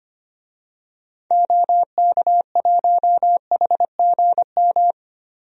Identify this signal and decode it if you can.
Morse code, heard as a keyed tone, about 25 wpm, 700 Hz. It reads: OK1HGM
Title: DXCC-100 Call Signs - 6 Characters - Repeated 2x - 25wpm